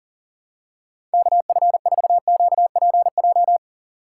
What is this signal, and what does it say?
Morse code, heard as a keyed tone, about 37 wpm, 700 Hz. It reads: KF4QPJ
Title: USA Call Signs - 6 Characters - 3x Character Spacing - 40wpm